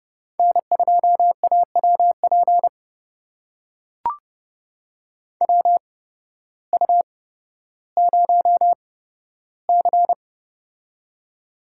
Morse code, heard as a keyed tone, about 30 wpm, spaced as for 14 wpm, 700 Hz. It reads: N2AWP WU0C
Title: USA Call Signs - 8x Character Spacing - 30wpm